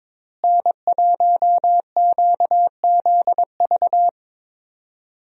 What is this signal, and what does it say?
Morse code, heard as a keyed tone, about 23 wpm, 700 Hz. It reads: N1QZV
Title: USA Call Signs - 5 Characters - Repeated 2x - 22wpm